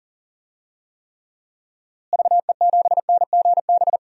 Morse code, heard as a keyed tone, about 37 wpm, 700 Hz. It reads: VE7NGB
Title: DXCC-100 Call Signs - 6 Characters - Repeated 2x - 40wpm